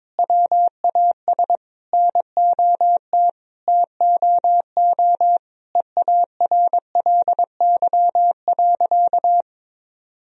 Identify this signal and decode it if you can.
Morse code, heard as a keyed tone, about 22 wpm, 700 Hz. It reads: WAS NOT TOO EARLY.
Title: Sentences from Top 1000 Words - Review Part 2 - 22wpm